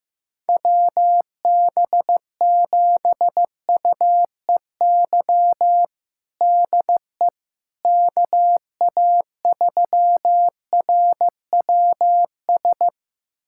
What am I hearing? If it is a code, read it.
Morse code, heard as a keyed tone, about 16 wpm, 700 Hz. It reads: WB7UEY DE KA3RWS
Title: USA Callsign DE Callsign - 6 Characters - 15wpm